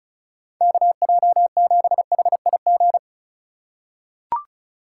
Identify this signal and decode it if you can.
Morse code, heard as a keyed tone, about 34 wpm, 700 Hz. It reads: KJ7HIG
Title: USA Call Signs - 6 Characters - 4x Character Spacing - 35wpm